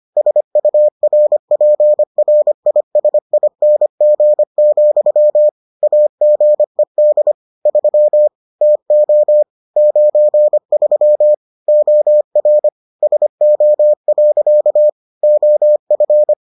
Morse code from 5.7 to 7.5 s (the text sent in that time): AGED